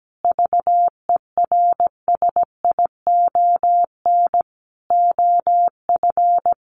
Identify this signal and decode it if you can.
Morse code, heard as a keyed tone, about 17 wpm, 700 Hz. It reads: VERSION OF
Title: Sentences from Top 1400 Words 17wpm